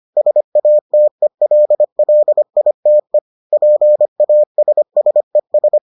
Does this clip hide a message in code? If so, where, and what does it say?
Yes: Morse code from 0.2 to 5.8 s, heard as a keyed tone, about 25 wpm, 600 Hz. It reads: SATELLITE PASSES